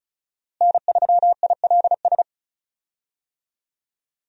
Morse code, heard as a keyed tone, about 34 wpm, 700 Hz. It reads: N3ILS
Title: USA Call Signs - 5 Characters - Repeated 2x - 35wpm